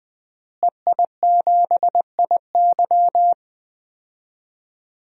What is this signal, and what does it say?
Morse code, heard as a keyed tone, about 21 wpm, 700 Hz. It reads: EI7IY
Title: DXCC-100 Call Signs - 5 Characters - Repeated 3x - 20wpm